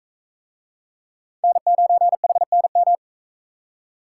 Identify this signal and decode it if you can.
Morse code, heard as a keyed tone, about 44 wpm, 700 Hz. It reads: N9HNM